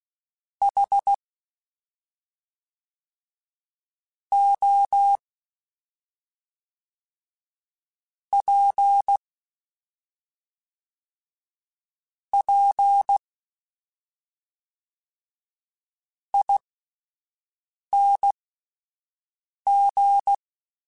Morse code, heard as a keyed tone, about 16 wpm, spaced as for 5 wpm, 800 Hz. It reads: H O P P ING
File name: Misc / Misc-mp3s / Jumps @5